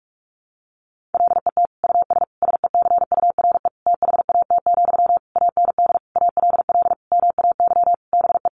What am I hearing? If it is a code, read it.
Morse code, heard as a keyed tone, about 44 wpm, 700 Hz. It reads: LET US SECURE THAT, AND ALL MAY BE
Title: Sentences from Top 1200 Words 45wpm